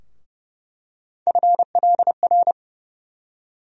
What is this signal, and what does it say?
Morse code, heard as a keyed tone, about 30 wpm, 700 Hz. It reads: FLR
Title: USA Call Sign Postfixes - 3 Letters 30wpm